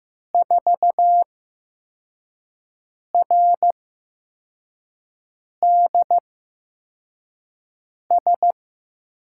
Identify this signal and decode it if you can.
Morse code, heard as a keyed tone, about 15 wpm, spaced as for 4 wpm, 700 Hz. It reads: 4RDS
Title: USA Call Signs - 6 Characters - 8x Character Spacing - 15wpm